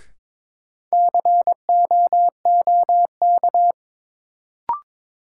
Morse code, heard as a keyed tone, about 21 wpm, 700 Hz. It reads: COOK
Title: Sentences from Top 700 Words - New Word Drill - 22wpm